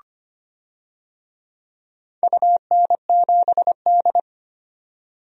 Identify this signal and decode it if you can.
Morse code, heard as a keyed tone, about 25 wpm, 700 Hz. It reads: UN7D
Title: DXCC-100 Call Signs - 4 Characters - 25wpm